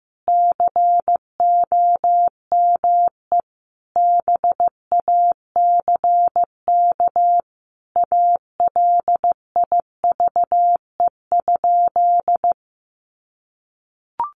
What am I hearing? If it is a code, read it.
Morse code, heard as a keyed tone, about 15 wpm, 700 Hz. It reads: COME BACK ALIVE?
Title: A Sound Of Thunder 15wpm